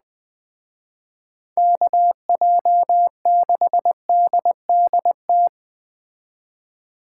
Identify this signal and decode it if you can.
Morse code, heard as a keyed tone, about 20 wpm, 700 Hz. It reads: KJ6DDT
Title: Speed Racing - USA 2x3 Call Signs 17wpm